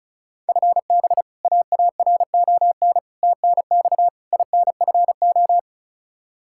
Morse code, heard as a keyed tone, about 36 wpm, 700 Hz. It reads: FB AARON TNX INFO
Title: K1USN SST QSO Practice 35wpm